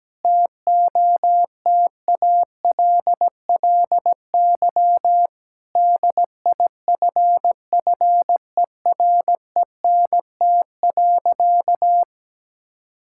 Morse code, heard as a keyed tone, about 17 wpm, 700 Hz. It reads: TOTALLY DIFFERENT.
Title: Sentences from Top 1600 Words 17wpm